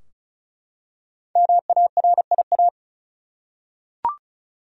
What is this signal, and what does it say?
Morse code, heard as a keyed tone, about 36 wpm, 700 Hz. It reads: MARIA
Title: USA Common Names - Max 5 Letters - 35wpm